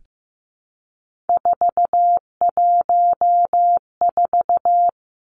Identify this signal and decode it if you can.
Morse code, heard as a keyed tone, about 15 wpm, 700 Hz. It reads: 414
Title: Numbers - 3-Digit 15wpm